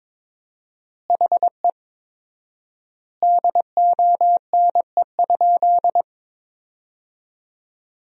Morse code, heard as a keyed tone, about 22 wpm, 700 Hz. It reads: HE DONE?
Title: 4x Word-Spacing - Sentences from Top 500 Words - 22wpm